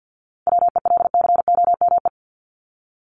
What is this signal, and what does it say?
Morse code, heard as a keyed tone, about 50 wpm, 700 Hz. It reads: WELCOME
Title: Top 1751-2000 Words 50wpm